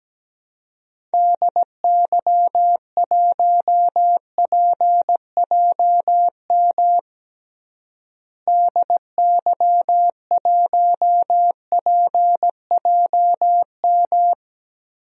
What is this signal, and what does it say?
Morse code, heard as a keyed tone, about 17 wpm, 700 Hz. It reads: DY1PJM DY1PJM
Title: DXCC-100 Call Signs - 6 Characters - Repeated 3x - 17wpm